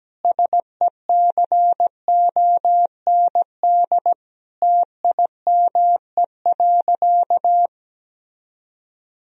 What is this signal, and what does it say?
Morse code, heard as a keyed tone, about 17 wpm, 700 Hz. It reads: SECOND TIME.
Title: Sentences from Top 1000 Words - Review Part 1 - 17wpm